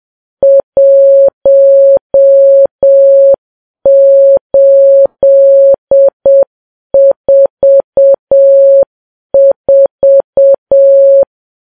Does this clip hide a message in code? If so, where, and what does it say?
Yes: Morse code from 0.4 to 11.2 s, heard as a keyed tone, about 7 wpm, 550 Hz. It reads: 1844